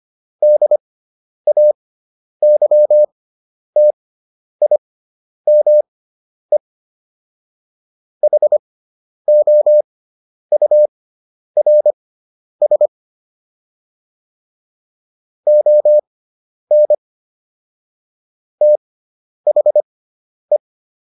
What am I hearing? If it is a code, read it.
Morse code, heard as a keyed tone, about 25 wpm, spaced as for 10 wpm, 600 Hz. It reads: DAYTIME HOURS ON THE